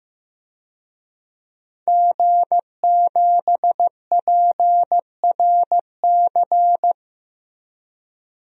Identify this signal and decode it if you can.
Morse code, heard as a keyed tone, about 15 wpm, 700 Hz. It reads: G7PRC
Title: DXCC-100 Call Signs - 5 Characters - Repeated 2x - 15wpm